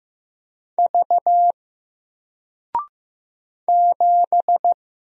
Morse code, heard as a keyed tone, about 15 wpm, 700 Hz. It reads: V 7
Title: Single Character (Letter:Number:Punctuation:Prosign) 15wpm